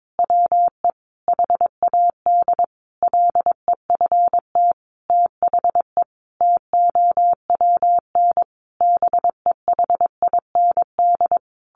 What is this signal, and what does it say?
Morse code, heard as a keyed tone, about 22 wpm, 700 Hz. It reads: WE HAD LEFT THE TOWN BEHIND
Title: Sentences from Top 1300 Words 22wpm